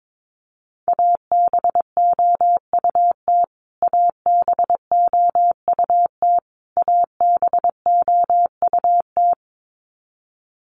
Morse code, heard as a keyed tone, about 22 wpm, 700 Hz. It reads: ABOUT ABOUT ABOUT
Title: Top 200 Words Repeated 22wpm